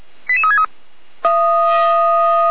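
Five-tone selective call (ZVEI), heard as quick runs of five tones, tones of about 70 ms. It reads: 88363
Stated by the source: ZVEI five-tone selective call received by SDR, Feuerwehr Orlamünde fire brigade.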